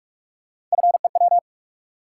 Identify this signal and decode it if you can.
Morse code, heard as a keyed tone, about 46 wpm, 700 Hz. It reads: FEW
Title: Sentences from Top 300 Words - New Word Drill - 45wpm